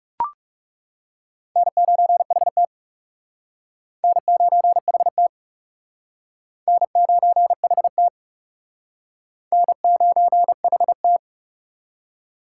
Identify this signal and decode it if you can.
Morse code, heard as a keyed tone, about 38 wpm, 700 Hz. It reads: N9HT N9HT N9HT N9HT
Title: Speed Racing - USA 1x2 Call Signs 30wpm